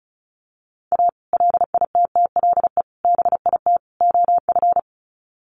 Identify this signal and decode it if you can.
Morse code, heard as a keyed tone, about 34 wpm, 700 Hz. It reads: A LITTLE BIT OF
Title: Sets of 4 Words 35wpm